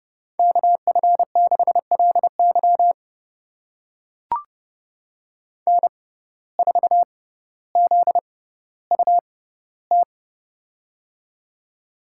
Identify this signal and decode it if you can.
Morse code, heard as a keyed tone, about 30 wpm, spaced as for 15 wpm, 700 Hz. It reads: KF6LY N4ZUT
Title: USA Call Signs - 5 Characters - 6x Character Spacing - 30wpm